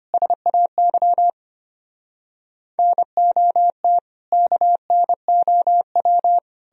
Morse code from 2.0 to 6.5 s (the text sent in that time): NOT KNOW